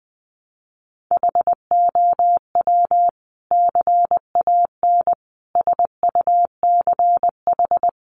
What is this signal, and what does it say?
Morse code, heard as a keyed tone, about 20 wpm, 700 Hz. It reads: HOW CAN SUCH
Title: Sentences from Top 1100 Words 20wpm